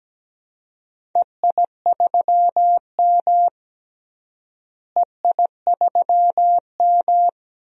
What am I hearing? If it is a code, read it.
Morse code, heard as a keyed tone, about 17 wpm, 700 Hz. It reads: EI3M EI3M
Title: DXCC-100 Call Signs - 4 Characters - Repeated 3x - 17wpm